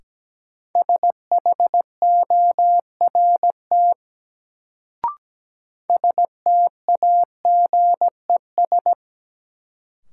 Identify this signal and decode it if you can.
Morse code, heard as a keyed tone, about 17 wpm, 700 Hz. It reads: SHORT STAGES
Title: Sets of 5 Words Encore - New Word Drill - 17wpm